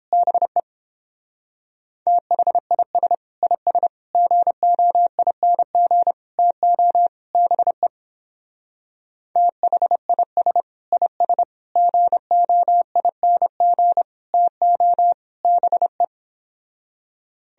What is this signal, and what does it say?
Morse code, heard as a keyed tone, about 29 wpm, 700 Hz. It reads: BE THIS IS GOING TO BE THIS IS GOING TO BE